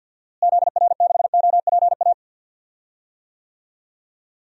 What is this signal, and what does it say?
Morse code, heard as a keyed tone, about 50 wpm, 700 Hz. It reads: ZR6OPA
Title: DXCC-100 Call Signs - 6 Characters - Repeated 2x - 50wpm